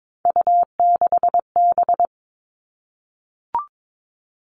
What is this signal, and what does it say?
Morse code, heard as a keyed tone, about 22 wpm, 700 Hz. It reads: U6B
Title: All Cut Numbers - 3-Digit 22wpm